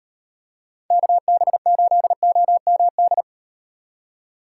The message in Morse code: KB8OMD

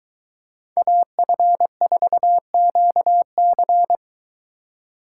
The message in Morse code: AF4QC

